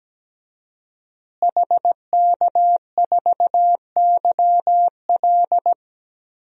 Morse code, heard as a keyed tone, about 17 wpm, 700 Hz. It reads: HK4YL
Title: DXCC-100 Call Signs - 5 Characters - Repeated 3x - 17wpm